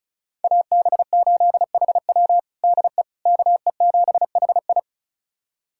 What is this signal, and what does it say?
Morse code, heard as a keyed tone, about 36 wpm, 700 Hz. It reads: AB8HW DE KE7HI
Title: USA Callsign DE Callsign - 5 Characters - 35wpm